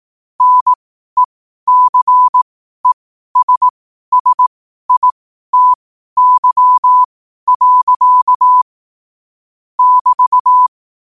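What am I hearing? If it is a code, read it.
Morse code, heard as a keyed tone, about 18 wpm, spaced as for 12 wpm, 1000 Hz. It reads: NECESSITY. <BT>
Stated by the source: Morse code quotes at 12 words per minute.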